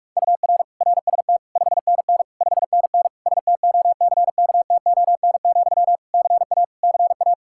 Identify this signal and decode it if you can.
Morse code, heard as a keyed tone, about 46 wpm, 700 Hz. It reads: UR RST 5NN 5NN STOCKTON, CA CA
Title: Practice QSO - Exchange 1 - Beginning - 45wpm